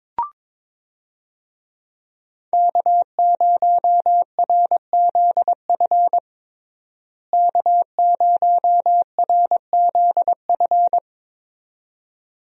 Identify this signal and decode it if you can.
Morse code, heard as a keyed tone, about 22 wpm, 700 Hz. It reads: K0RZF K0RZF
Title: USA Call Signs - 5 Characters - Repeated 2x - 22wpm